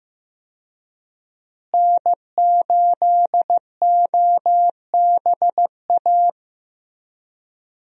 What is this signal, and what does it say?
Morse code, heard as a keyed tone, about 15 wpm, 700 Hz. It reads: N8OBA
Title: USA Call Signs - 5 Characters - Repeated 3x - 15wpm